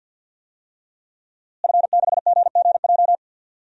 Morse code, heard as a keyed tone, about 51 wpm, 700 Hz. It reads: F6GGJ
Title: DXCC-100 Call Signs - 5 Characters - Repeated 2x - 50wpm